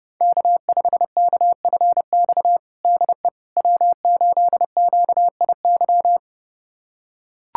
Morse code, heard as a keyed tone, about 30 wpm, 700 Hz. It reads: K5KFX DE W8QIY
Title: USA Callsign DE Callsign - 5 Characters - 30wpm